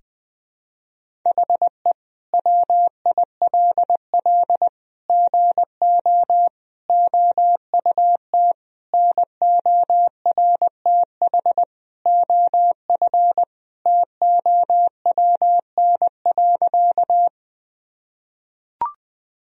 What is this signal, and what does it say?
Morse code, heard as a keyed tone, about 20 wpm, 700 Hz. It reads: HE WILL GO OUT NORTH OF TOWN.
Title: Sentences from Top 1000 Words - Review Part 1 - 20wpm